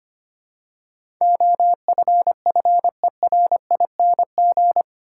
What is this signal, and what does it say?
Morse code, heard as a keyed tone, about 25 wpm, 700 Hz. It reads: OFFERING
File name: 8 Letter Words - First 3 Letters Given - 25wpm